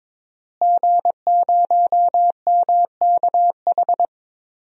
Morse code, heard as a keyed tone, about 22 wpm, 700 Hz. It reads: G0MKH
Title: DXCC-100 Call Signs - 5 Characters - Repeated 2x - 22wpm